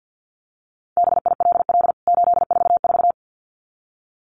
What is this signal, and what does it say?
Morse code, heard as a keyed tone, about 49 wpm, 700 Hz. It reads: BILL 844
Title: CWT Exchange 50wpm